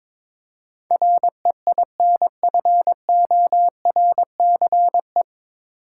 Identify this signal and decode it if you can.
Morse code, heard as a keyed tone, about 22 wpm, 700 Hz. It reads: REINFORCE
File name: 9 Letter Words - First 3 Letters Given - 22wpm